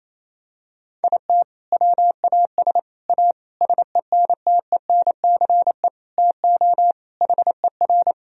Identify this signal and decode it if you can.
Morse code, heard as a keyed tone, about 28 wpm, 700 Hz. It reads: IT WAS A SENTENCE TO HER
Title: Sentences from Top 1000 Words - Review Part 1 - 28wpm